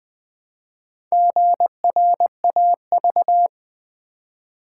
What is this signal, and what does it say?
Morse code, heard as a keyed tone, about 20 wpm, 700 Hz. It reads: GRAV
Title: Build A Word - 1 Letter At A Time - 5-Letter Words - 20wpm